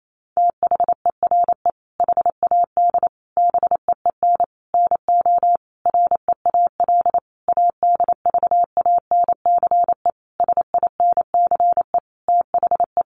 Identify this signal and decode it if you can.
Morse code, heard as a keyed tone, about 28 wpm, 700 Hz. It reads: THERE HAD BEEN NO REAL ADVANCE SINCE THE
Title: Sentences from Top 1400 Words 28wpm